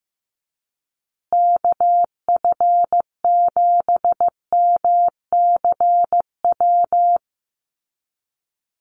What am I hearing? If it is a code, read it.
Morse code, heard as a keyed tone, about 15 wpm, 700 Hz. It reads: KF7MCW
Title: USA Call Signs Repeated 2x 15wpm